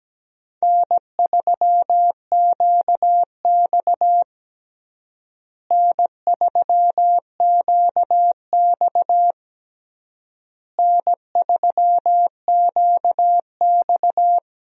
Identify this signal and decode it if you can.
Morse code, heard as a keyed tone, about 17 wpm, 700 Hz. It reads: N3QX N3QX N3QX
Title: USA Call Signs - 4 Characters - Repeated 3x - 17wpm